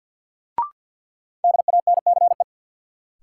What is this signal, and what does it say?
Morse code, heard as a keyed tone, about 48 wpm, 700 Hz. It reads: DANCE
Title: Sentences from Top 700 Words - New Word Drill - 50wpm